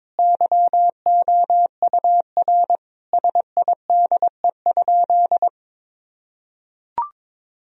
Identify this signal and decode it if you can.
Morse code, heard as a keyed tone, about 22 wpm, 700 Hz. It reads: YOUR SIDE?
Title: Sentences from Top 1000 Words - Review Part 1 - 22wpm